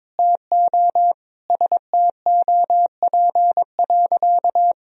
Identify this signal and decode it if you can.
Morse code, heard as a keyed tone, about 21 wpm, 700 Hz. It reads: TO STOP.
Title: Sentences from Top 1000 Words - Review Part 2 - 22wpm